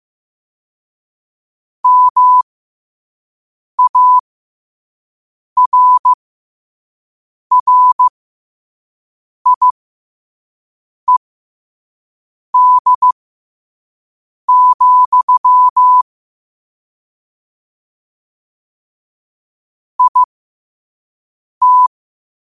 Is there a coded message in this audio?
Yes: Morse code from 1.8 to 21.9 s, heard as a keyed tone, about 15 wpm, spaced as for 5 wpm, 1000 Hz. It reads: MARRIED, IT